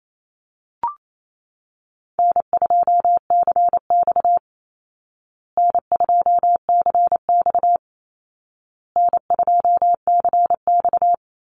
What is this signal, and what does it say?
Morse code, heard as a keyed tone, about 28 wpm, 700 Hz. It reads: N2CX N2CX N2CX
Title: Call Signs Repeated 2x - 28wpm